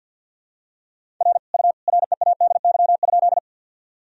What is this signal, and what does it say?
Morse code, heard as a keyed tone, about 46 wpm, 700 Hz. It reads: R U READY?
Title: Sentences from Top 400 Words and Abbreviations 50wpm